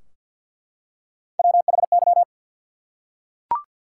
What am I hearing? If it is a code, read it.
Morse code, heard as a keyed tone, about 48 wpm, 700 Hz. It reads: WHY
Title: Three Letter Words - 50wpm